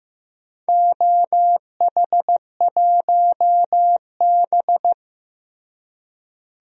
Morse code, heard as a keyed tone, about 15 wpm, 700 Hz. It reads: OH1B
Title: DXCC-100 Call Signs - 4 Characters - Repeated 3x - 15wpm